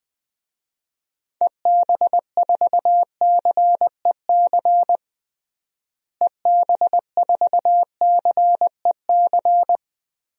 Morse code, heard as a keyed tone, about 20 wpm, 700 Hz. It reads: EB4CEC EB4CEC
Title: DXCC-100 Call Signs - 6 Characters - Repeated 3x - 20wpm